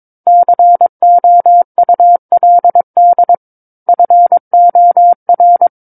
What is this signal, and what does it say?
Morse code, heard as a keyed tone, about 22 wpm, 700 Hz. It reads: COULD FOR
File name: 06-Chapter-06 22 WPM